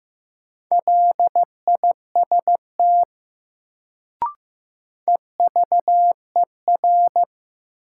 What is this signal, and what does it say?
Morse code, heard as a keyed tone, about 16 wpm, 700 Hz. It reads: LIST EVER
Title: Sentences from Top 400 Words - New Word Drill - 15wpm